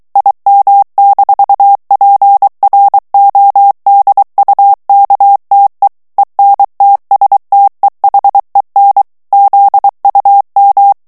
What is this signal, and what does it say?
Morse code, heard as a keyed tone, about 23 wpm, 800 Hz. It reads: IM-PRODUKTE ENTSTEHEN ZUM
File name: morse mp3 / ALLEMAND texte clair 1 vitesse 22 mots minute